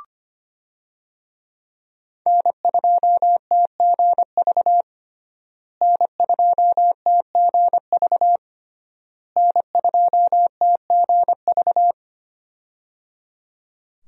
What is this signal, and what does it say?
Morse code, heard as a keyed tone, about 25 wpm, 700 Hz. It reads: N2TGV N2TGV N2TGV
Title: USA Call Signs - 5 Characters - Repeated 3x - 25wpm